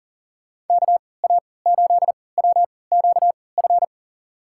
Morse code, heard as a keyed tone, about 40 wpm, spaced as for 23 wpm, 700 Hz. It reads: KA8WQF
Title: USA Call Signs - 6 Characters - 3x Character Spacing - 40wpm